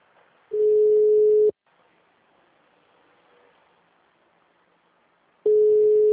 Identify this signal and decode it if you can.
Ringback tone (Europe, 425 Hz).
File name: out- -2008- - - .13